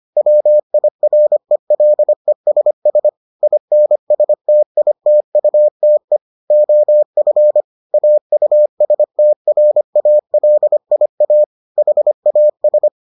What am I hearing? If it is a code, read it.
Morse code, heard as a keyed tone, about 25 wpm, 600 Hz. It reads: WIRELESS INSTITUTE OF AUSTRALIA HAS